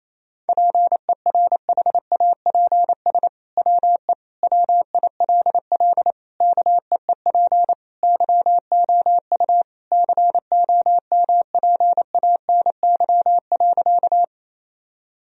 Morse code, heard as a keyed tone, about 28 wpm, 700 Hz. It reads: PERHAPS WE WILL KEEP YOU COMPANY.
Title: Sentences from Top 1000 Words - Review Part 3 - 28wpm